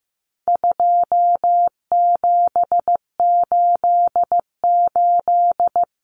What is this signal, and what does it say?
Morse code, heard as a keyed tone, about 15 wpm, 700 Hz. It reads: 2788
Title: Common Cut Numbers - 4-Digit 15wpm